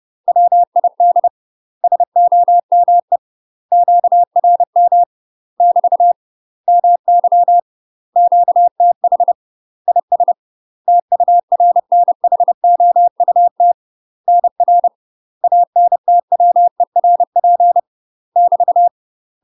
Morse code, heard as a keyed tone, about 30 wpm, 700 Hz. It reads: WID SOME QRM <BT> MY QTH IS TURNHOUT NR ANTWERP <BT>
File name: UK MASTER QSOs TRAINING / 30WPM / UK MASTER QSOs 0038